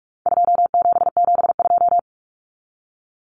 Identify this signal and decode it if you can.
Morse code, heard as a keyed tone, about 44 wpm, 700 Hz. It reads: 2772
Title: Common Cut Numbers - 4-Digit 45wpm